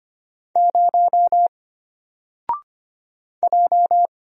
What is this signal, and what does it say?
Morse code, heard as a keyed tone, about 25 wpm, 700 Hz. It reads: 0 J
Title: Single Letter-Number 25wpm